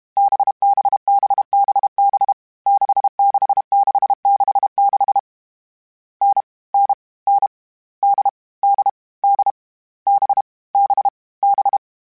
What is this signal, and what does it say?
Morse code, heard as a keyed tone, about 32 wpm, 800 Hz. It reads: BBBBB 66666 N N N D D D B B B
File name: Difficult Combos / Difficult Combos-mp3s / NDB NDB6 @30